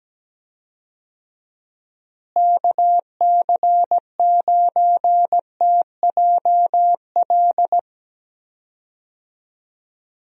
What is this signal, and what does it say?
Morse code, heard as a keyed tone, about 17 wpm, 700 Hz. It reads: KC9TJL
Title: USA Call Signs - 6 Characters - Repeated 3x - 17wpm